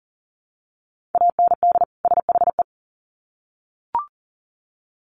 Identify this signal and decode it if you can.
Morse code, heard as a keyed tone, about 40 wpm, 700 Hz. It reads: AND SHE
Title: Sets of 2 Words 40wpm